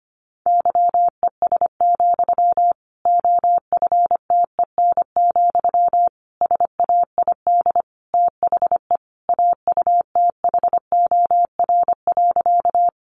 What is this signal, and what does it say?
Morse code, heard as a keyed tone, about 25 wpm, 700 Hz. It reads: YES, OFTEN, SAID THE AUTHOR.
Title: Sentences from Top 1200 Words 25wpm